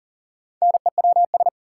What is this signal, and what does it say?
Morse code, heard as a keyed tone, about 40 wpm, 700 Hz. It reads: NEWS
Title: Sentences from Top 800 Words - New Word Drill - 40wpm